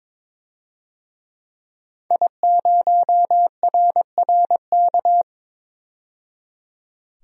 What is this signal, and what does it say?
Morse code, heard as a keyed tone, about 22 wpm, 700 Hz. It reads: I0RRK
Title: DXCC-100 Call Signs - 5 Characters - 22wpm